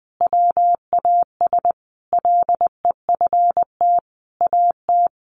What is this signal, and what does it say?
Morse code, heard as a keyed tone, about 20 wpm, 700 Hz. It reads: WAS LEFT AT
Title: Sentences from Top 1400 Words 20wpm